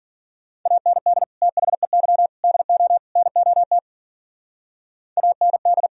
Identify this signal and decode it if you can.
Morse code, heard as a keyed tone, about 52 wpm, 700 Hz. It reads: AND THEY DO NOT AND